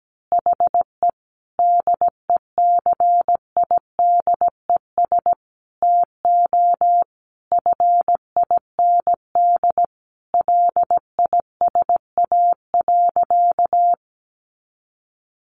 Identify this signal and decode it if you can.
Morse code, heard as a keyed tone, about 17 wpm, 700 Hz. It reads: HE DECIDES TO FIND LISA.
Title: Additional Easy Phrases 17wpm